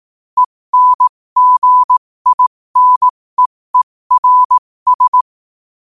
Morse code, heard as a keyed tone, about 18 wpm, spaced as for 15 wpm, 1000 Hz. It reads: ENGINEERS
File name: qotdfiles / 15wpm